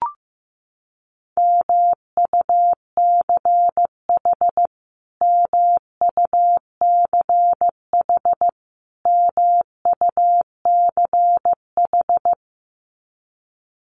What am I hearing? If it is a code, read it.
Morse code, heard as a keyed tone, about 15 wpm, 700 Hz. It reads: MUCH MUCH MUCH
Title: Top 500 Words Repeated 15wpm